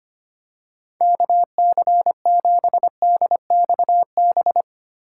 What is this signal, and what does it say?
Morse code, heard as a keyed tone, about 25 wpm, 700 Hz. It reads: KC7DXB
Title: USA Call Signs - 6 Characters - 4x Character Spacing - 25wpm